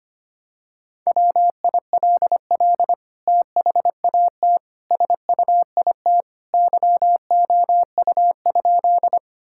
Morse code, heard as a keyed tone, about 25 wpm, 700 Hz. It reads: WILL THAT SUIT YOU?
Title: Sentences from Top 1600 Words 25wpm